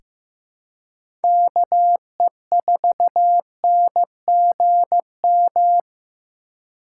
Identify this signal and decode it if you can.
Morse code, heard as a keyed tone, about 15 wpm, 700 Hz. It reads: KE4NGM
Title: USA Call Signs - 3x Character Spacing - 15wpm